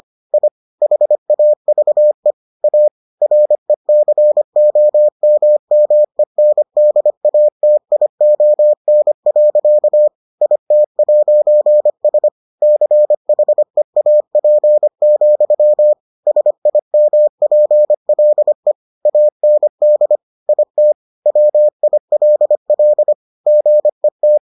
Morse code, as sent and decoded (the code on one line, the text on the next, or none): .. / .... .- ...- . / .- / .-. -.-. --- -- -- . -. -.. .- - .. --- -. .-.-.- / .. - .----. ... / -.-. .... .- .--. --..-- / ... .. -- .--. .-.. / .- -. -.. / .. - / .-- .. .-.. .-.. / --. -
I HAVE A RECOMMENDATION. IT'S CHEAP, SIMPLE AND IT WILL GET